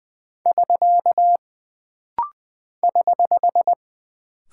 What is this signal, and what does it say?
Morse code, heard as a keyed tone, about 20 wpm, 700 Hz. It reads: <SK> <HH>
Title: Single Punctuation-Prosign - Rapid-Fire - 20wpm